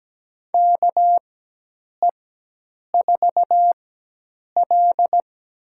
Morse code, heard as a keyed tone, about 17 wpm, spaced as for 8 wpm, 700 Hz. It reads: KE4L
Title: USA Call Signs - 4x Character Spacing - 17wpm